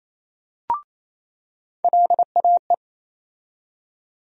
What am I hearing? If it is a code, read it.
Morse code, heard as a keyed tone, about 28 wpm, 700 Hz. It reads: LAE
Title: USA Call Sign Postfixes - 3 Letters 28wpm